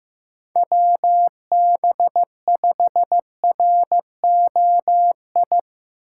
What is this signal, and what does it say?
Morse code, heard as a keyed tone, about 17 wpm, 700 Hz. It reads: WB5ROI
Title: Speed Racing - USA 2x3 Call Signs 15wpm